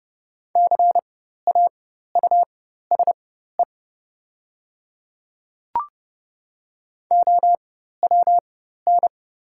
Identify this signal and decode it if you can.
Morse code, heard as a keyed tone, about 30 wpm, spaced as for 14 wpm, 700 Hz. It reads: CAUSE OWN
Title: 4x Character Spacing - Top 300 Words - 30wpm